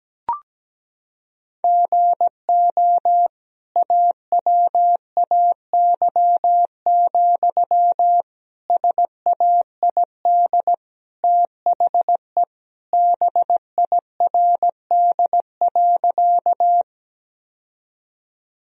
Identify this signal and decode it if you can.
Morse code, heard as a keyed tone, about 17 wpm, 700 Hz. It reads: GO AWAY, SAID THE BIRD.
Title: Sentences from Top 1000 Words - Review Part 2 - 17wpm